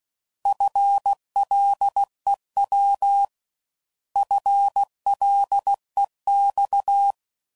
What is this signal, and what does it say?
Morse code, heard as a keyed tone, about 16 wpm, 800 Hz. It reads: FLEW FLEX